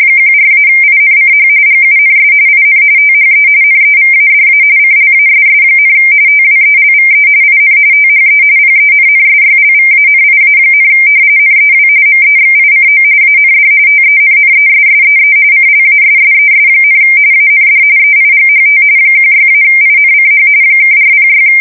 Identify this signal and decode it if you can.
RTTY, about 45.45 baud. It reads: 14-INCH GUNS IN THE NEVADA CLASS, TWELVE IN THE PENNSYLVANIA, NEW MEXICO AND TENNESSEE CLASSES, AND EIGHT 16-INCH (406
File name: fsk170 / battleships